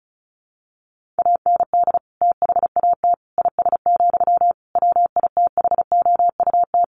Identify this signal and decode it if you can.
Morse code, heard as a keyed tone, about 34 wpm, 700 Hz. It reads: AND THAT IS, WITHOUT
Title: Sentences from Top 1500 Words 35wpm